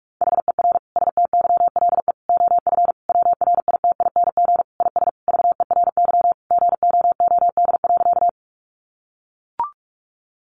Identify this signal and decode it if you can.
Morse code, heard as a keyed tone, about 44 wpm, 700 Hz. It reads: HER STYLE OF WRITING IS VERY GOOD.
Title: Sentences from Top 800 Words 45wpm